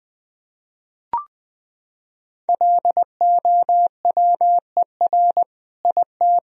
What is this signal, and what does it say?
Morse code, heard as a keyed tone, about 20 wpm, 700 Hz. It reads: LOWER IT
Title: Sentences from Top 1000 Words - Review Part 4 - 20wpm